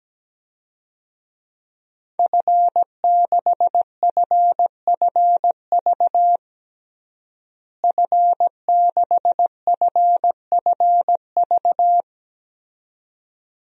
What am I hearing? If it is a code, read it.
Morse code, heard as a keyed tone, about 17 wpm, 700 Hz. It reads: F6FFV F6FFV
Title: DXCC-100 Call Signs - 5 Characters - Repeated 2x - 17wpm